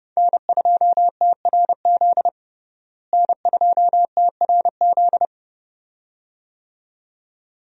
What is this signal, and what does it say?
Morse code, heard as a keyed tone, about 32 wpm, 700 Hz. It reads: N2TRZ N2TRZ
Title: USA Call Signs - 5 Characters - Repeated 2x - 30wpm